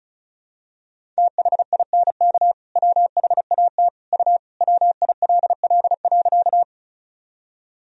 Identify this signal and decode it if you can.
Morse code, heard as a keyed tone, about 36 wpm, 690 Hz. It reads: THINK WHAT U WILL.
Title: Sentences from Top 200 Words and Abbreviations 35wpm